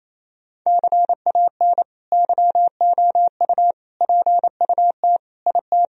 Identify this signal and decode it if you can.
Morse code, heard as a keyed tone, about 28 wpm, 700 Hz. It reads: CAN YOU PUT IT
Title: Sentences from Top 1000 Words - Review Part 4 - 28wpm